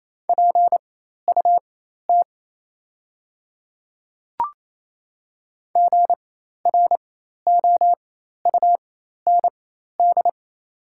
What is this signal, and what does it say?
Morse code, heard as a keyed tone, about 28 wpm, spaced as for 13 wpm, 690 Hz. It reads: PUT GROUND
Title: 4x Character Spacing - Top 500 Words - 28wpm